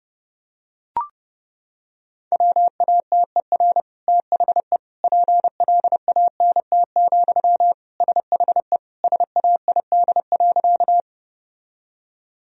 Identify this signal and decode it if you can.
Morse code, heard as a keyed tone, about 30 wpm, 700 Hz. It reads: WATER THE PLANT, SHE SAID.
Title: Sentences from Top 1000 Words - Review Part 1 - 30wpm